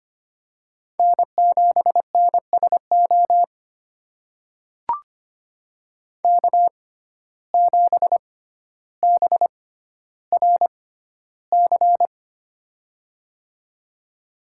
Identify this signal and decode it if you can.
Morse code, heard as a keyed tone, about 25 wpm, spaced as for 13 wpm, 700 Hz. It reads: N7NSO K7BRC
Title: USA Call Signs - 5 Characters - 6x Character Spacing - 25wpm